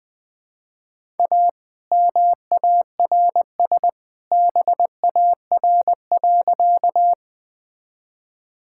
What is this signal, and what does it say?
Morse code, heard as a keyed tone, about 20 wpm, 700 Hz. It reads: A MARS BAR.
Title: All-Age Appropriate Jokes - 20wpm